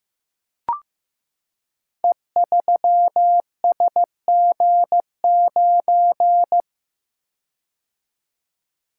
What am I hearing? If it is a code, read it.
Morse code, heard as a keyed tone, about 15 wpm, 700 Hz. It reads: E3SG9
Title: Code Groups - 5 Letter:Numbers - 15wpm